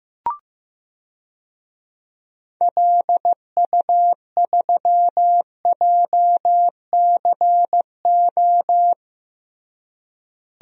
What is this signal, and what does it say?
Morse code, heard as a keyed tone, about 15 wpm, 700 Hz. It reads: LU3JCO
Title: DXCC-100 Call Signs - 6 Characters - Repeated 3x - 15wpm